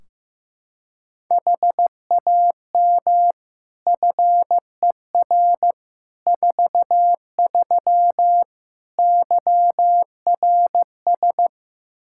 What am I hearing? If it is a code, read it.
Morse code, heard as a keyed tone, about 15 wpm, 700 Hz. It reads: HAM FER 43 YRS
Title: Practice QSO - Exchange 3 - More Info - 15wpm